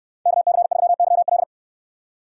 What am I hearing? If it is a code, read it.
Morse code, heard as a keyed tone, about 69 wpm, 700 Hz. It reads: KC3YL